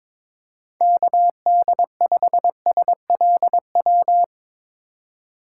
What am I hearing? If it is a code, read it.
Morse code, heard as a keyed tone, about 21 wpm, 700 Hz. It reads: KD5SLW
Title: USA Call Signs - 3x Character Spacing - 22wpm